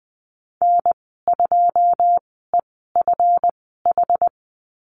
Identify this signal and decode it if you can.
Morse code, heard as a keyed tone, about 20 wpm, spaced as for 15 wpm, 700 Hz. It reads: N2EFH
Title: 179 Lesson 29g - Slash - Call Signs 20wpm